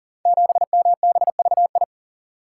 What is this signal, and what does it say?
Morse code, heard as a keyed tone, about 42 wpm, 700 Hz. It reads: 7MBVI
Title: Code Groups - 5 Letter:Numbers - 40wpm